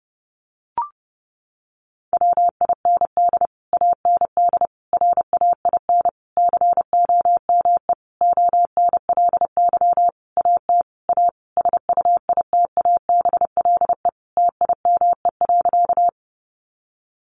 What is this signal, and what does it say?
Morse code, heard as a keyed tone, about 30 wpm, 700 Hz. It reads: WIND AND RAIN COME ONLY AT A SUITABLE TIME.
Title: Sentences from Top 1400 Words 30wpm